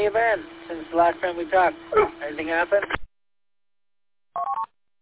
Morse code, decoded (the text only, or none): none